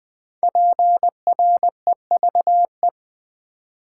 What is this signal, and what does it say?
Morse code, heard as a keyed tone, about 20 wpm, 700 Hz. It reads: PREVE
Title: Build A Word - 5 Letters At A Time - 10-Letter Words - 20wpm